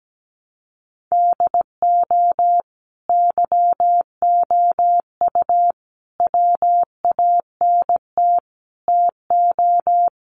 Morse code, heard as a keyed tone, about 17 wpm, 700 Hz. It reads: DO YOU WANT TO
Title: Sets of 4 Words 17wpm